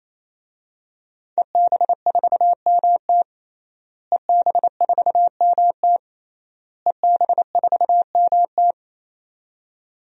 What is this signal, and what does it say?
Morse code, heard as a keyed tone, about 28 wpm, 700 Hz. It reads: EB4MT EB4MT EB4MT
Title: DXCC-100 Call Signs - 5 Characters - Repeated 3x - 28wpm